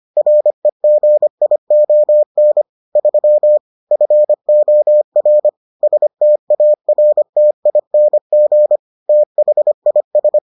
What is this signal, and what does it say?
Morse code, heard as a keyed tone, about 25 wpm, 600 Hz. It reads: REGION 3 FOR STARTING THIS